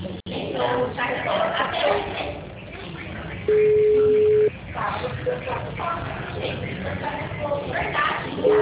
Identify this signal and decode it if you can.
Ringback tone (Europe, 425 Hz), with speech.